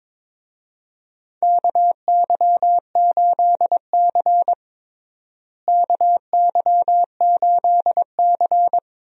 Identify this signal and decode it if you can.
Morse code, heard as a keyed tone, about 22 wpm, 700 Hz. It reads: KY8C KY8C
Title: USA Call Signs - 4 Characters - Repeated 3x - 22wpm